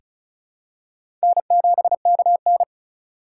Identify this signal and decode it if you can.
Morse code, heard as a keyed tone, about 36 wpm, 700 Hz. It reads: N7KN
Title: USA Call Signs - 3x Character Spacing - 35wpm